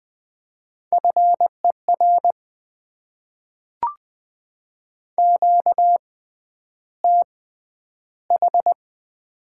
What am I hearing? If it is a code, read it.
Morse code, heard as a keyed tone, about 20 wpm, spaced as for 10 wpm, 690 Hz. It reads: FER QTH
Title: Top 20 QSO Elements - 6x Character Spacing - 20wpm